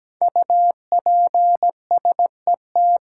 Morse code, heard as a keyed tone, about 17 wpm, 700 Hz. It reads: UPSET
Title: Sentences from Top 800 Words - New Word Drill - 17wpm